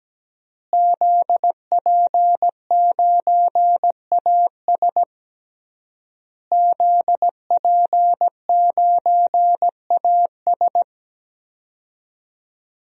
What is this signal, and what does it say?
Morse code, heard as a keyed tone, about 17 wpm, 700 Hz. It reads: ZP9AS ZP9AS
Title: DXCC-100 Call Signs - 5 Characters - Repeated 3x - 17wpm